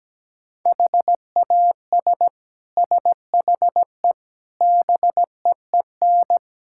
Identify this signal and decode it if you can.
Morse code, heard as a keyed tone, about 17 wpm, 700 Hz. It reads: HAS SHE BEEN
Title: Sentences from Top 1000 Words - Review Part 6 - 17wpm